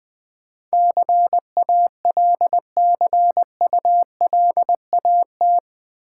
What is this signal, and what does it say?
Morse code, heard as a keyed tone, about 20 wpm, 700 Hz. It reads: CALCULAT
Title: Build A Word - 4 Letters At A Time - 11-Letter Words - 20wpm